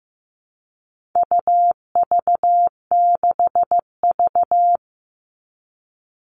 Morse code, heard as a keyed tone, about 15 wpm, 700 Hz. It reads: UV6V
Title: All Cut Numbers - 4-Digit 15wpm